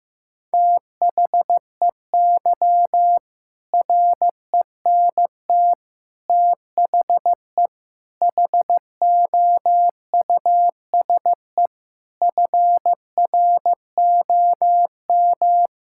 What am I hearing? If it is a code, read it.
Morse code, heard as a keyed tone, about 15 wpm, 700 Hz. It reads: THEY RENT THE HOUSE FROM
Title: Sentences from Top 1000 Words - Review Part 5 - 15wpm